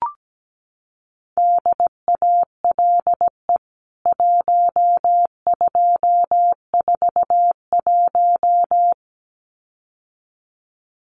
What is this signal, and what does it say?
Morse code, heard as a keyed tone, about 17 wpm, 700 Hz. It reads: DALE 1241
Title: CWT Exchange 17wpm